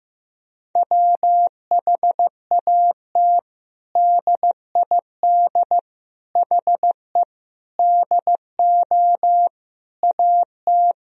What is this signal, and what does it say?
Morse code, heard as a keyed tone, about 15 wpm, 700 Hz. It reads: WHAT DID HE DO AT
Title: Sentences from Top 1000 Words - Review Part 1 - 15wpm